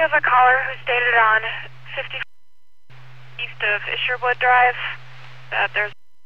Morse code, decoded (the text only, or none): none